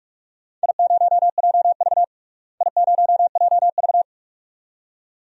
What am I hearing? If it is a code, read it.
Morse code, heard as a keyed tone, about 46 wpm, 700 Hz. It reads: I0JV I0JV
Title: DXCC-100 Call Signs - 4 Characters - Repeated 2x - 45wpm